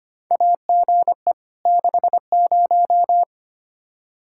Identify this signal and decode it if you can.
Morse code, heard as a keyed tone, about 25 wpm, 700 Hz. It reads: AGE 60
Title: Practice QSO - Exchange 3 - More Info - 25wpm